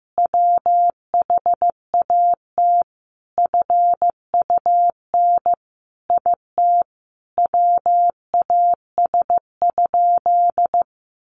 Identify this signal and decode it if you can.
Morse code, heard as a keyed tone, about 15 wpm, 700 Hz. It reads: WHAT FUN IT WAS?
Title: Sentences from Top 700 Words 15wpm